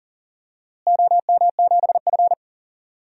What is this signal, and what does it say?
Morse code, heard as a keyed tone, about 40 wpm, 700 Hz. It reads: OM7F
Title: DXCC-100 Call Signs - 4 Characters - Repeated 2x - 40wpm